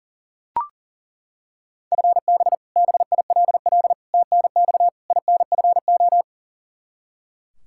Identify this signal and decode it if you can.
Morse code, heard as a keyed tone, about 37 wpm, 700 Hz. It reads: FB BILL TNX INFO
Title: Practice QSO - Exchange 3 - More Info - 40wpm